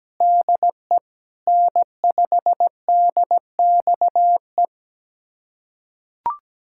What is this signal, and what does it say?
Morse code, heard as a keyed tone, about 17 wpm, 700 Hz. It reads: DE N5DXE
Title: USA Callsign DE Callsign - 5 Characters - 17wpm